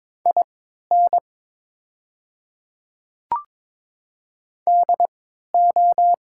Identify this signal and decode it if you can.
Morse code, heard as a keyed tone, about 22 wpm, spaced as for 12 wpm, 700 Hz. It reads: IN DO
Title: 3x Character Spacing - Top 500 Words - 22wpm